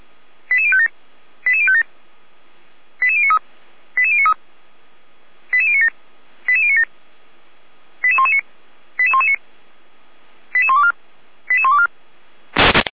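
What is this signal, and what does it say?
Five-tone selective call (ZVEI), heard as quick runs of five tones, tones of about 70 ms. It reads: 88057 88057 88093 88093 88097 88097 88119 88119 88124 88124
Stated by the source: SDR-received ZVEI five-tone selective call, fire brigade Feuerwehr Orlamünde.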